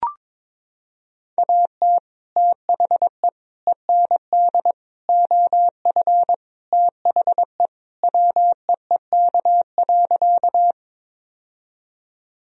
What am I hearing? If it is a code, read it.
Morse code, heard as a keyed tone, about 22 wpm, 700 Hz. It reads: AT THE END OF THE WEEK.
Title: Sentences from Top 1000 Words - Review Part 2 - 22wpm